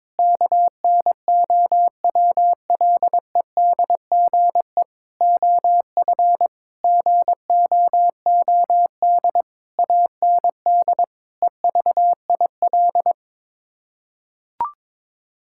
Morse code, heard as a keyed tone, about 22 wpm, 700 Hz. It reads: KNOWLEDGE OF GOOD AND EVIL
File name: Sets of 9 Words - 22wpm